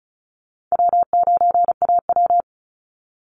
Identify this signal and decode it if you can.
Morse code, heard as a keyed tone, about 34 wpm, 700 Hz. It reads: W9AW
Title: Call Signs - 35wpm